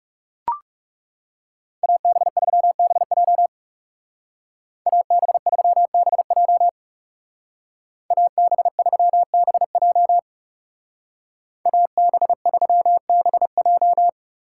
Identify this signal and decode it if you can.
Morse code, heard as a keyed tone, about 37 wpm, 700 Hz. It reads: AB3BJ AB3BJ AB3BJ AB3BJ
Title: Speed Racing - USA 2x2 Call Signs 30wpm